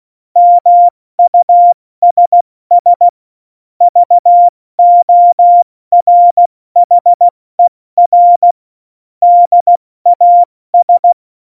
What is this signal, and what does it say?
Morse code, heard as a keyed tone, about 16 wpm, spaced as for 14 wpm, 700 Hz. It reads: MUSS VORHER DAS
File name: Morsekurs / Witze / Witz 22. Dezember 2020, 70 F80